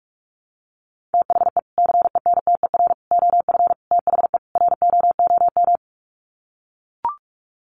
Morse code, heard as a keyed tone, about 44 wpm, 700 Hz. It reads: THE CENTER OF THE ROOM
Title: Sets of 5 Words Encore 45wpm